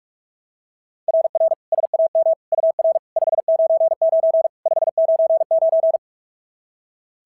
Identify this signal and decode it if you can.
Morse code, heard as a keyed tone, about 41 wpm, 690 Hz. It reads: RR SAM UR 599 599
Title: SOTA QSO Practice 45wpm